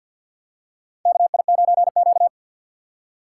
Morse code, heard as a keyed tone, about 52 wpm, 700 Hz. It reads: KI9Q
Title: USA Call Signs - 4 Characters - 3x Character Spacing - 50wpm